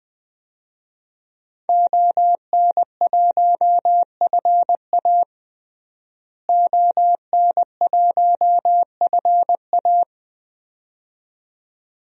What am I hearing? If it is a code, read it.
Morse code, heard as a keyed tone, about 20 wpm, 700 Hz. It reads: ON1FA ON1FA
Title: DXCC-100 Call Signs - 5 Characters - Repeated 2x - 20wpm